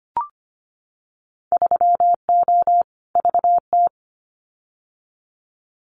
Morse code, heard as a keyed tone, about 25 wpm, 700 Hz. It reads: 3O VT
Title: Winter Field Day Exchanges Encore 25wpm